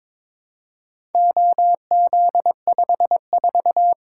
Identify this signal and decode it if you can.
Morse code, heard as a keyed tone, about 22 wpm, 700 Hz. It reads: OZ54
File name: practice / 22wpm / Four Letter-Number Combinations - 22wpm